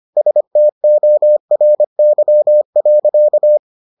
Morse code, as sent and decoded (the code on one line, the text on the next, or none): ... - --- .-. -.-- .-.-.-
STORY.